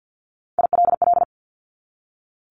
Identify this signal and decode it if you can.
Morse code, heard as a keyed tone, about 50 wpm, 700 Hz. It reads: ILL
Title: Top Three Letter Combinations in Words 50wpm